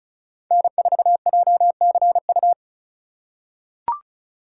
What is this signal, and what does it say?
Morse code, heard as a keyed tone, about 36 wpm, 700 Hz. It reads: N4JCU
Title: USA Call Signs - 5 Characters - 3x Character Spacing - 35wpm